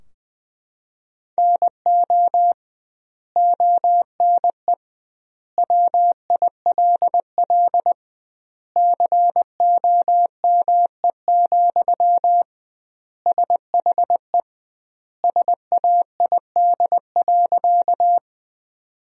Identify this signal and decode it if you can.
Morse code, heard as a keyed tone, about 21 wpm, 700 Hz. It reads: NO ONE WILL COME, SHE SAID.
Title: 2x Word-Spacing - Sentences from Top 100 Words - 20wpm